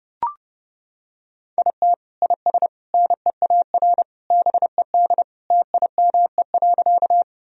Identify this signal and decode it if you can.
Morse code, heard as a keyed tone, about 32 wpm, 700 Hz. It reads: IT IS NEAR BED TIME.
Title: Sentences from Top 1000 Words - Review Part 1 - 30wpm